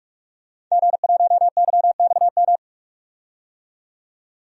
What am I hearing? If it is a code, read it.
Morse code, heard as a keyed tone, about 44 wpm, 700 Hz. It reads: G1YXM
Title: DXCC-100 Call Signs - 5 Characters - 45wpm